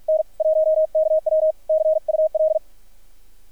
Morse code, heard as a keyed tone, about 50 wpm, spaced as for 42 wpm, 630 Hz. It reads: N1KW CUL
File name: archived / CRSnet